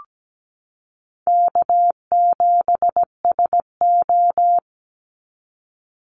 Morse code, heard as a keyed tone, about 17 wpm, 700 Hz. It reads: K7SO
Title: NA SOTA Activator Callsigns 17wpm